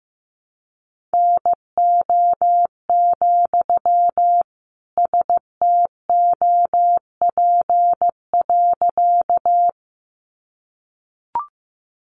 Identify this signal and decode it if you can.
Morse code, heard as a keyed tone, about 15 wpm, 700 Hz. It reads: NO, STOP.
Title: Sentences from Top 300 Words 15wpm